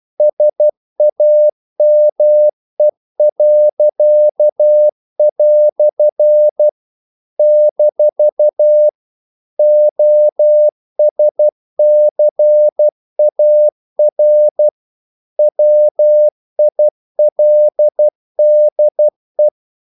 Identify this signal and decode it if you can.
Morse code, heard as a keyed tone, about 12 wpm, 600 Hz. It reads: SAME." - OSCAR WILDE
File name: CW / Practice / EnglishQuotations0312